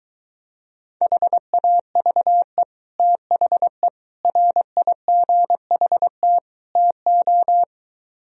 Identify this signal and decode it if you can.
Morse code, heard as a keyed tone, about 23 wpm, 700 Hz. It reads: HAVE THE RIGHT TO